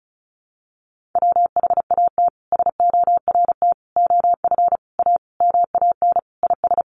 Morse code, heard as a keyed tone, about 34 wpm, 700 Hz. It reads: WHAT SORT OF A MAN IS
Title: Sentences from Top 1300 Words 35wpm